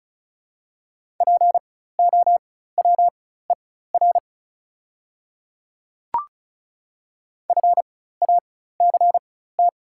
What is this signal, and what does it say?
Morse code, heard as a keyed tone, about 36 wpm, spaced as for 16 wpm, 690 Hz. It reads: POWER FACT
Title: 4x Character Spacing - Top 500 Words - 35wpm